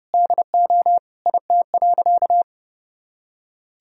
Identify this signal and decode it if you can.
Morse code, heard as a keyed tone, about 32 wpm, 700 Hz. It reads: DO IT.
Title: Sentences from Top 1000 Words - Review Part 2 - 30wpm